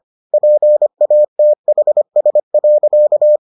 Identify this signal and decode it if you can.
Morse code, heard as a keyed tone, about 25 wpm, 600 Hz. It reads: PATHS.